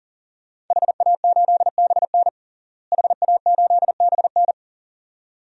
Morse code, heard as a keyed tone, about 40 wpm, 700 Hz. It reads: HA8BN HA8BN
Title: DXCC-100 Call Signs - 5 Characters - Repeated 2x - 40wpm